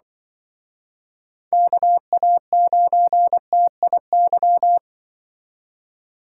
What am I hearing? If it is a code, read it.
Morse code, heard as a keyed tone, about 24 wpm, 700 Hz. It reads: KA9TIY
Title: Speed Racing - USA 2x3 Call Signs 20wpm